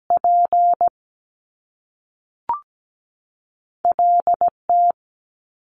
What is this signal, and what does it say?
Morse code, heard as a keyed tone, about 17 wpm, 700 Hz. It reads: P L T
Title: Top Two Letter Combinations in Words 17wpm